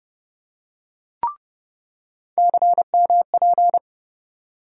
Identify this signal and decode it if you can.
Morse code, heard as a keyed tone, about 30 wpm, 700 Hz. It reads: CMP
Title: USA Call Sign Postfixes - 3 Letters 30wpm